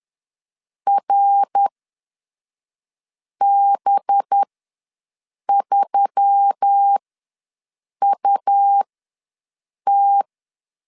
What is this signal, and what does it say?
Morse code, heard as a keyed tone, about 10 wpm, spaced as for 5 wpm, 800 Hz. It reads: RB3UT
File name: LEZIONI / 5 LBVC / 5 LBV ws 10wpm 3645